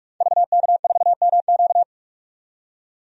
Morse code, heard as a keyed tone, about 43 wpm, 700 Hz. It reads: VK4MQ
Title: DXCC-100 Call Signs - 5 Characters - 45wpm